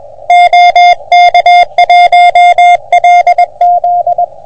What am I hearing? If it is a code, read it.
Morse code, heard as a keyed tone, about 21 wpm, 720 Hz. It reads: OK1LZ
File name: OM3PA PILUP CUC 388